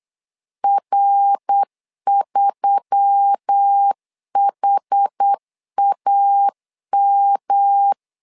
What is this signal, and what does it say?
Morse code, heard as a keyed tone, about 8 wpm, 800 Hz. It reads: R3HAM
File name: LEZIONI / 4 Q1S3 8wpm 3037